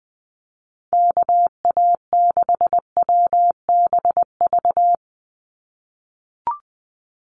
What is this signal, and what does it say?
Morse code, heard as a keyed tone, about 20 wpm, 700 Hz. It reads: KA6WBV
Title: USA Call Signs 20wpm